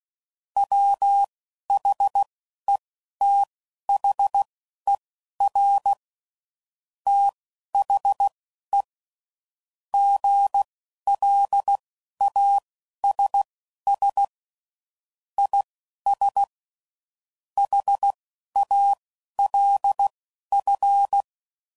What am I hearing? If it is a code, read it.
Morse code, heard as a keyed tone, about 16 wpm, spaced as for 11 wpm, 800 Hz. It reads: WHETHER THE GLASS IS HALF